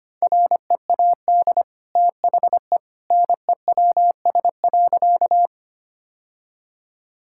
Morse code, heard as a keyed tone, about 25 wpm, 700 Hz. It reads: READ THE NEWS.
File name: Easy Sentences with Hints - 25wpm